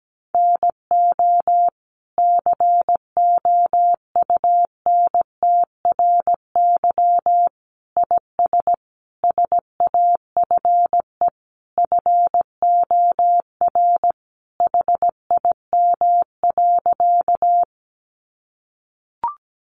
Morse code, heard as a keyed tone, about 17 wpm, 700 Hz. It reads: NO COUNTRY IS SAFE FOR HIM.
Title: Sentences from Top 900 Words 17wpm